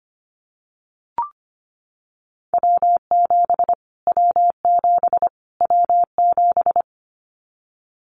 Morse code, heard as a keyed tone, about 25 wpm, 700 Hz. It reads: W7 W7 W7
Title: Callsign Prefixes for Top 25 DXCC - Repeated 3x - 25wpm